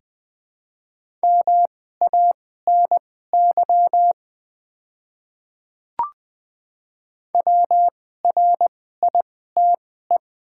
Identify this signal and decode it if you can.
Morse code, heard as a keyed tone, about 20 wpm, spaced as for 14 wpm, 700 Hz. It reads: MANY WRITE
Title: 2x Character Spacing - Top 100 Words - 20wpm